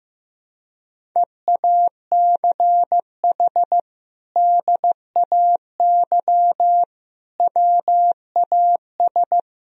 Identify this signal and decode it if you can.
Morse code, heard as a keyed tone, about 16 wpm, 700 Hz. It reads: EACH DAY WAS
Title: Sentences from Top 1000 Words - Review Part 1 - 15wpm